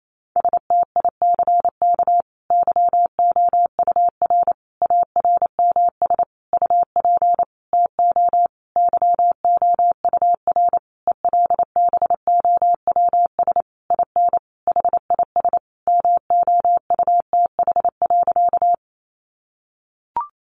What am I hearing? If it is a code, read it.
Morse code, heard as a keyed tone, about 28 wpm, 700 Hz. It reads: STICK YOUR ARMS UP TO YOUR ELBOWS IN HIS MOUTH.
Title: A Sound Of Thunder 28wpm